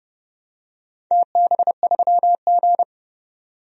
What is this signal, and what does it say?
Morse code, heard as a keyed tone, about 30 wpm, 700 Hz. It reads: TB3G
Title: DXCC-100 Call Signs - 4 Characters - 30wpm